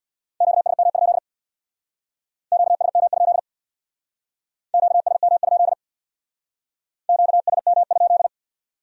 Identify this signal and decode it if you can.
Morse code, heard as a keyed tone, about 61 wpm, 700 Hz. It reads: QSM? QSM? QSM? QSM?
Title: Speed-Racing Q-Codes - 50wpm